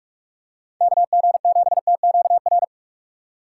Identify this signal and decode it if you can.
Morse code, heard as a keyed tone, about 44 wpm, 700 Hz. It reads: KG7TQR
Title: USA Call Signs - 6 Characters - 6x Character Spacing - 45wpm